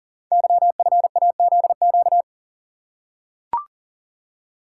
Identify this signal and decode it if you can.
Morse code, heard as a keyed tone, about 38 wpm, 700 Hz. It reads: YFAZQ
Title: Code Groups - 5 Letter:Numbers - 40wpm